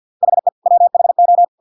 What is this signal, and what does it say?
Morse code, heard as a keyed tone, about 50 wpm, 700 Hz. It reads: HE WHO